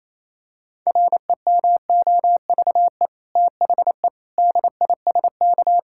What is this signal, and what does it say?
Morse code, heard as a keyed tone, about 28 wpm, 700 Hz. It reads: REMOVE THE DISK
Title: Sentences from Top 1000 Words - Review Part 6 - 28wpm